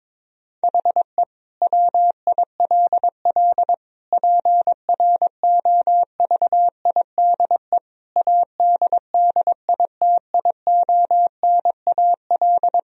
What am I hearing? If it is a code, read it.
Morse code, heard as a keyed tone, about 22 wpm, 700 Hz. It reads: HE WILL PROVIDE ADDITIONAL
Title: Sentences from Top 1000 Words - Review Part 4 - 22wpm